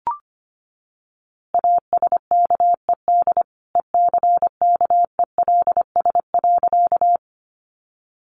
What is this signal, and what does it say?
Morse code, heard as a keyed tone, about 25 wpm, 700 Hz. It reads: ASKED ECKELS.
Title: A Sound Of Thunder 25wpm